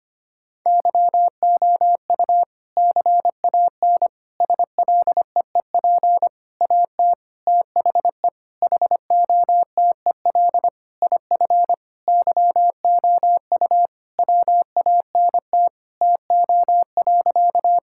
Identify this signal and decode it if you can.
Morse code, heard as a keyed tone, about 25 wpm, 700 Hz. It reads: YOU CAN SLEEP AT THE HOTEL IF YOU WANT TO.
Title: Sentences from Top 1600 Words 25wpm